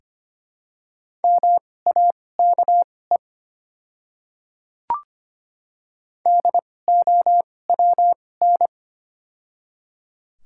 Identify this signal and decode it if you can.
Morse code, heard as a keyed tone, about 25 wpm, spaced as for 18 wpm, 700 Hz. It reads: MAKE DOWN
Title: 2x Character Spacing - Top 100 Words - 25wpm